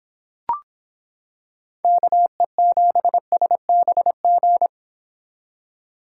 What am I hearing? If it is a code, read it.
Morse code, heard as a keyed tone, about 25 wpm, 700 Hz. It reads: KE7SBG